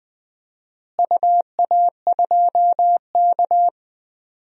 Morse code, heard as a keyed tone, about 21 wpm, 700 Hz. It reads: UA2K
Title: DXCC-100 Call Signs - 4 Characters - Repeated 3x - 20wpm